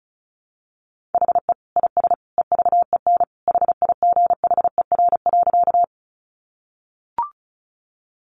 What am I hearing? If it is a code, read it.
Morse code, heard as a keyed tone, about 34 wpm, 700 Hz. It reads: HE IS EVEN HIGHER.
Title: Additional Easy Phrases 35wpm